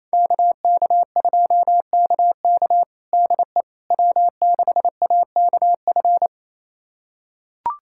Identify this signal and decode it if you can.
Morse code, heard as a keyed tone, about 28 wpm, 700 Hz. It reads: KK2KK DE W6AKF
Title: USA Callsign DE Callsign - 5 Characters - 28wpm